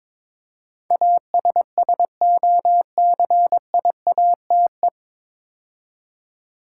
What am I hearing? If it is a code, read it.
Morse code, heard as a keyed tone, about 22 wpm, 700 Hz. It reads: ASSOCIATE
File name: 9 Letter Words - First 3 Letters Given - 22wpm